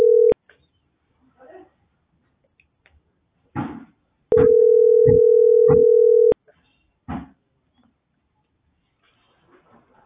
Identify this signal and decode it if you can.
Ringback tone (North America), 440 + 480 Hz.